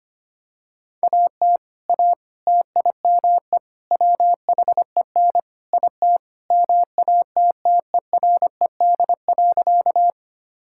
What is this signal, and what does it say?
Morse code, heard as a keyed tone, about 25 wpm, 700 Hz. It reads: AT A TIME WHEN IT MATTERED.
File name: practice / 25wpm / Sentences with 4-Word Phrases - 25wpm